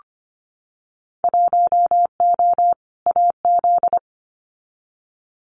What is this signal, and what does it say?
Morse code, heard as a keyed tone, about 25 wpm, 700 Hz. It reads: 1O AZ
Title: Winter Field Day Exchanges 25wpm